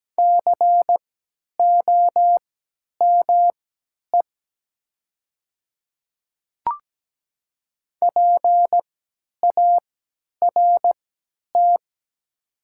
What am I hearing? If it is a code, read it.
Morse code, heard as a keyed tone, about 17 wpm, spaced as for 10 wpm, 700 Hz. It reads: COME PART
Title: 3x Character Spacing - Top 200 Words - 17wpm